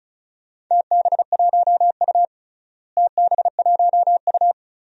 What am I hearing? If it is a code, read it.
Morse code, heard as a keyed tone, about 36 wpm, 700 Hz. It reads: TB1U TB1U
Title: DXCC-100 Call Signs - 4 Characters - Repeated 2x - 35wpm